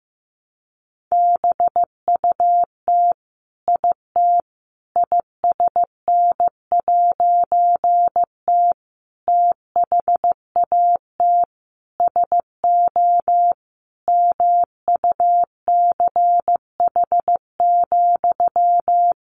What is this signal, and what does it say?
Morse code, heard as a keyed tone, about 15 wpm, 700 Hz. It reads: BUT IT ISN'T THAT SO MUCH,
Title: A Sound Of Thunder 15wpm